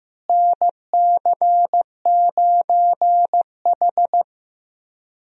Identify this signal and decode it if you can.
Morse code, heard as a keyed tone, about 15 wpm, 700 Hz. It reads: NC9H
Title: USA Call Signs - 4 Characters - Repeated 3x - 15wpm